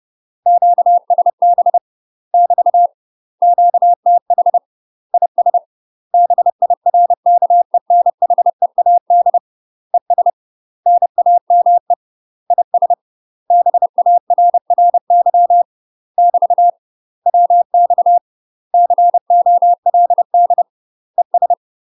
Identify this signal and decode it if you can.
Morse code, heard as a keyed tone, about 30 wpm, 700 Hz. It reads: QSB <BT> QTH IS BIRKENHEAD ES NAME IS BARRY <BT> WX COLD ES
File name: UK MASTER QSOs TRAINING / 30WPM / UK MASTER QSOs 0003